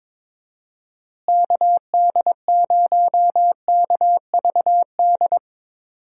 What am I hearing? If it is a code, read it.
Morse code, heard as a keyed tone, about 22 wpm, 700 Hz. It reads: KD0KVD
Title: USA Call Signs - 6 Characters - Repeated 2x - 22wpm